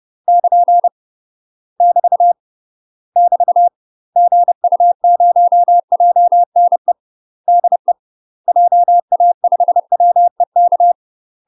Morse code, heard as a keyed tone, about 30 wpm, 700 Hz. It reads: <KN> <BT> <BT> GU0JNE DE JA5WEK
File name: UK MASTER QSOs TRAINING / 30WPM / UK MASTER QSOs 0015